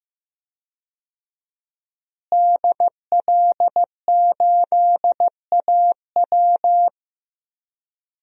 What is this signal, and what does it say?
Morse code, heard as a keyed tone, about 15 wpm, 700 Hz. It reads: DL8AW
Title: DXCC-100 Call Signs - 5 Characters - Repeated 2x - 15wpm